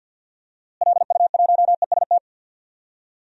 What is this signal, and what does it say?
Morse code, heard as a keyed tone, about 50 wpm, 700 Hz. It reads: LU1EST
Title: DXCC-100 Call Signs - 6 Characters - Repeated 3x - 50wpm